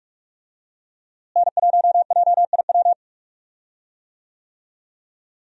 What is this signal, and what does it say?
Morse code, heard as a keyed tone, about 46 wpm, 700 Hz. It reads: N1JIW
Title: USA Call Signs - 5 Characters - Repeated 3x - 45wpm